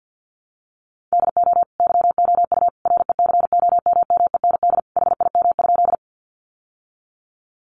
Morse code, heard as a keyed tone, about 48 wpm, 700 Hz. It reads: DO YOU RECOMMEND HIM?
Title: Sentences from Top 1200 Words 50wpm